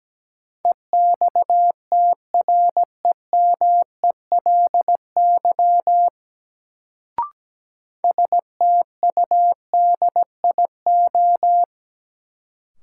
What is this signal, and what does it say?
Morse code, heard as a keyed tone, about 17 wpm, 700 Hz. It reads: EXTREMELY STUDIO
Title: Sentences from Top 1200 Words - New Word Drill Alt - 17wpm